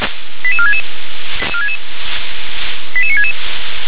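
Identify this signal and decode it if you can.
Five-tone selective call (ZVEI), heard as quick runs of five tones, tones of about 70 ms. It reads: 88466 88966
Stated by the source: ZVEI five-tone selective call 88465, received with an SDR.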